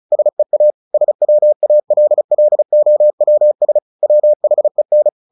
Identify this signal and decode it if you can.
Morse code, heard as a keyed tone, about 34 wpm, 600 Hz. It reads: SEA SWALLOWS WHEN
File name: the old man and the sea / the old man and the sea 1 35wpm